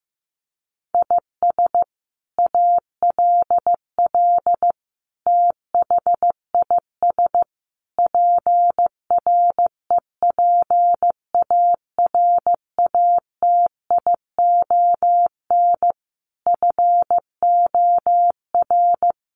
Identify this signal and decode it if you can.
Morse code, heard as a keyed tone, about 15 wpm, 700 Hz. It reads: IS ALL THIS PREPARATION FOR
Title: Sentences from Top 1400 Words 15wpm